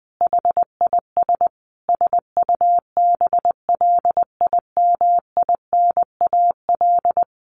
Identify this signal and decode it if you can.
Morse code, heard as a keyed tone, about 20 wpm, 700 Hz. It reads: HIS SUBLIMINAL
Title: A Sound Of Thunder 20wpm